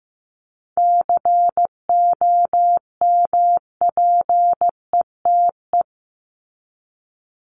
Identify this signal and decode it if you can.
Morse code, heard as a keyed tone, about 15 wpm, 700 Hz. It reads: COMPETE
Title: Top 1751-2000 Words 15wpm